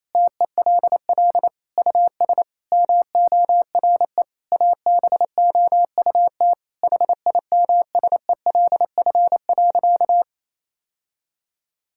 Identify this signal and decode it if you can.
Morse code, heard as a keyed tone, about 28 wpm, 700 Hz. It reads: TELL US MORE ABOUT HIMSELF.
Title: Sentences from Top 1600 Words 28wpm